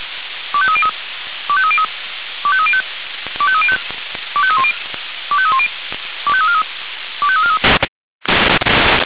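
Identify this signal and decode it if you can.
Five-tone selective call (ZVEI), heard as quick runs of five tones, tones of about 70 ms. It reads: 25303 25303 25305 25305 25310 25310 25343 25343
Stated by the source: ZVEI five-tone call received on SDR.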